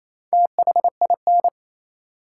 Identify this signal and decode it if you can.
Morse code, heard as a keyed tone, about 28 wpm, 700 Hz. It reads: THIN
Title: Sentences from Top 1200 Words - New Word Drill - 28wpm